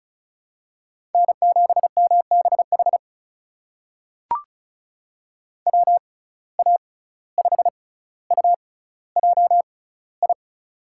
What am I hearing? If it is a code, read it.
Morse code, heard as a keyed tone, about 36 wpm, spaced as for 17 wpm, 700 Hz. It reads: N7MBH WA5UJI
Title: USA Call Signs - 6x Character Spacing - 35wpm